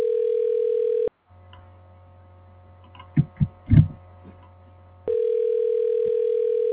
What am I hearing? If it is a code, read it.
Ringback tone (North America), 440 + 480 Hz.
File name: recordings / REb a f7fef2ba95a3